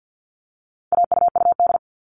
Morse code, heard as a keyed tone, about 48 wpm, 700 Hz. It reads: AUUD
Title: All Cut Numbers - 4-Digit 50wpm